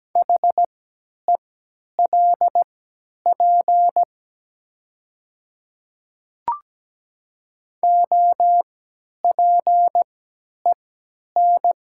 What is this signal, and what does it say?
Morse code, heard as a keyed tone, about 17 wpm, spaced as for 10 wpm, 700 Hz. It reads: HELP OPEN
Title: 3x Character Spacing - Top 300 Words - 17wpm